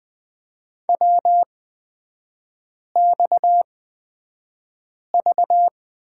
Morse code, heard as a keyed tone, about 20 wpm, spaced as for 5 wpm, 690 Hz. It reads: WXV
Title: 229 Lesson 37e - Letter x - Three Characters 20wpm